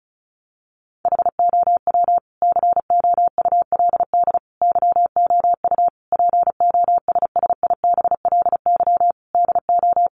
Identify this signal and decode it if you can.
Morse code, heard as a keyed tone, about 34 wpm, 700 Hz. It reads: HOW COULD YOU POSSIBLY DO
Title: Sentences from Top 1300 Words 35wpm